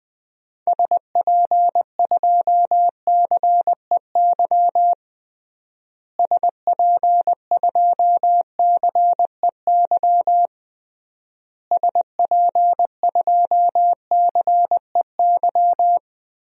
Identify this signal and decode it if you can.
Morse code, heard as a keyed tone, about 20 wpm, 700 Hz. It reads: SP2CEY SP2CEY SP2CEY
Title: DXCC-100 Call Signs - 6 Characters - Repeated 3x - 20wpm